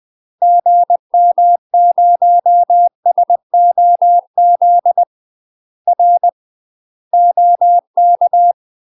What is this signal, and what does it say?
Morse code, heard as a keyed tone, about 20 wpm, 700 Hz. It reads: GM0SOZ R OK
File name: UK MASTER QSOs TRAINING / 20WPM / UK MASTER QSOs 0040